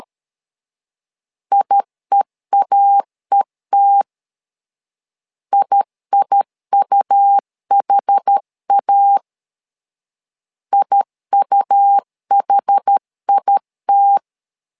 Morse code, heard as a keyed tone, about 12 wpm, 800 Hz. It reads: IEAET IIUHA IUHIT
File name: LEZIONI / 2 UHDN / 2 UH 12wpm 1439